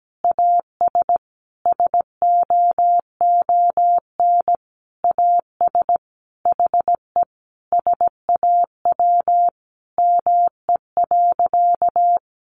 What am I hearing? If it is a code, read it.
Morse code, heard as a keyed tone, about 17 wpm, 700 Hz. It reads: AS SOON AS HE SAW ME.
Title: Sentences from Top 1100 Words 17wpm